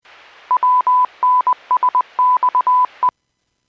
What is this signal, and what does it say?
Morse code, heard as a keyed tone, about 20 wpm, 1000 Hz. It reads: WNSXE
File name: WPS / 00337 14-29 WPS 451.25 WPS